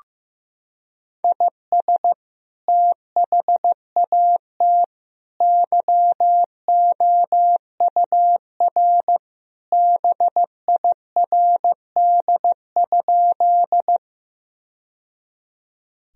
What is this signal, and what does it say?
Morse code, heard as a keyed tone, about 15 wpm, 700 Hz. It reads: IS THAT YOUR BIRD?
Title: Sentences from Top 1000 Words - Review Part 2 - 15wpm